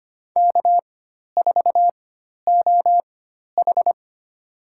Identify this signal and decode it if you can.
Morse code, heard as a keyed tone, about 25 wpm, spaced as for 12 wpm, 700 Hz. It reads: K4OH
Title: USA Call Signs - 4 Characters - 4x Character Spacing - 25wpm